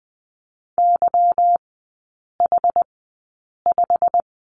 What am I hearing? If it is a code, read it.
Morse code, heard as a keyed tone, about 20 wpm, spaced as for 8 wpm, 690 Hz. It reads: YH5
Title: 145 Lesson 24e - Letter y - Three Characters 20wpm